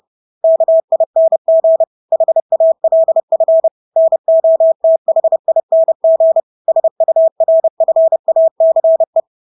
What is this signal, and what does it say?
Morse code, heard as a keyed tone, about 30 wpm, 650 Hz. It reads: KING HALF NOTHING SURFACE
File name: Engelske ord 30-30 wpm